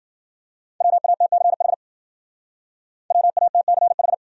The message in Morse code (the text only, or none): WATCH WATCH